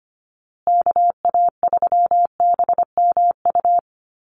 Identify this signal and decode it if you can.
Morse code, heard as a keyed tone, about 25 wpm, 700 Hz. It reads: KA3BMU
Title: Call Signs Repeated 2x - 25wpm